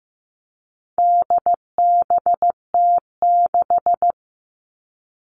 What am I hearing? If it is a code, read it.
Morse code, heard as a keyed tone, about 15 wpm, 700 Hz. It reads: DBT6
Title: All Cut Numbers - 4-Digit 15wpm